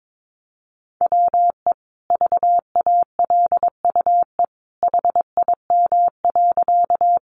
Morse code, heard as a keyed tone, about 22 wpm, 700 Hz. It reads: WE VALUE HIM.
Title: Sentences from Top 600 Words 22wpm